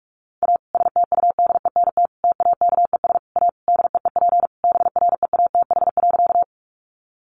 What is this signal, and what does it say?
Morse code, heard as a keyed tone, about 44 wpm, 700 Hz. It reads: A STUDENT TAKES A DEEP BREATH.
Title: Additional Easy Phrases 45wpm